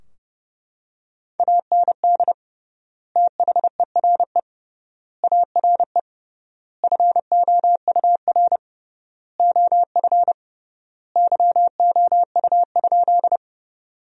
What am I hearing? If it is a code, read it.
Morse code, heard as a keyed tone, about 30 wpm, 700 Hz. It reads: AND THERE ARE FOUR OF YOU?
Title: 3x Word-Spacing - Sentences from Top 300 Words - 30wpm